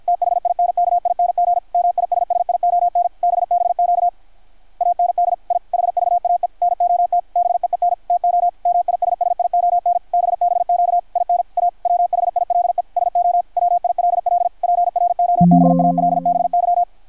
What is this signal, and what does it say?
Morse code, heard as a keyed tone, about 53 wpm, 660 Hz, with radio noise. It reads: THINKING MISSION BBQ AND I HVAE NOT BEEN TO MISSION BBQ IN A WHILE SO WILL PROBALBY
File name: archived / CRSnet